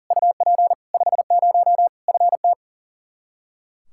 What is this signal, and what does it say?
Morse code, heard as a keyed tone, about 40 wpm, 700 Hz. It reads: UP 50 FT
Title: Practice QSO - Exchange 2 - Info - 40wpm